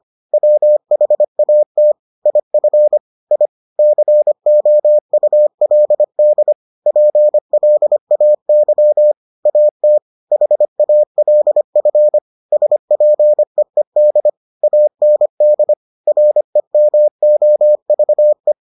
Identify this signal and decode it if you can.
Morse code, heard as a keyed tone, about 25 wpm, 600 Hz. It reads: WHAT IF I COULD PLAY AT HALF SPEED AND REMOVE